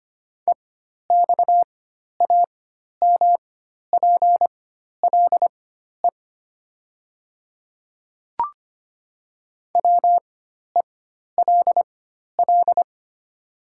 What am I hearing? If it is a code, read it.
Morse code, heard as a keyed tone, about 25 wpm, spaced as for 12 wpm, 700 Hz. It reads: EXAMPLE WELL
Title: 4x Character Spacing - Top 300 Words - 25wpm